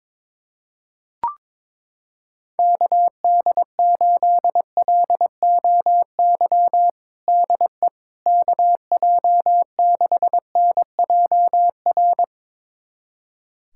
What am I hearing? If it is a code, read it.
Morse code, heard as a keyed tone, about 22 wpm, 700 Hz. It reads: KD8LOY DE KJ6NJR
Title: USA Callsign DE Callsign - 6 Characters - 22wpm